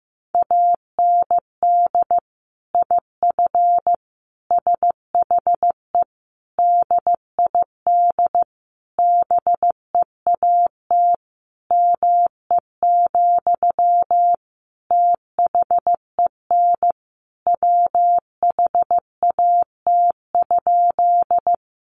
Morse code, heard as a keyed tone, about 15 wpm, 700 Hz. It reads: AND IF SHE DID BEAT ME, THEN WHAT?
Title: Sentences from Top 1300 Words 15wpm